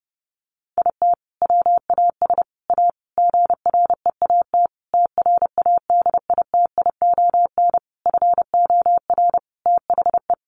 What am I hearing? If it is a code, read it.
Morse code, heard as a keyed tone, about 30 wpm, 700 Hz. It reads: IT WAS A GREAT TRADITION FOR THE
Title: Sentences from Top 1500 Words 30wpm